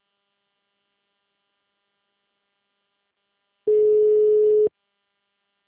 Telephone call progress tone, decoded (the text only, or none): ringback tone